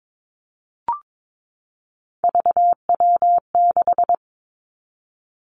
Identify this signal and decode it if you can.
Morse code, heard as a keyed tone, about 22 wpm, 700 Hz. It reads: VW6
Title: Three Challenging Letter-Number Combinations 22wpm